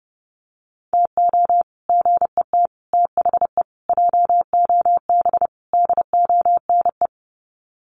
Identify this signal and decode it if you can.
Morse code, heard as a keyed tone, about 28 wpm, 700 Hz. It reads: TO GET THE JOB DONE
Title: Sets of 5 Words Encore2 30wpm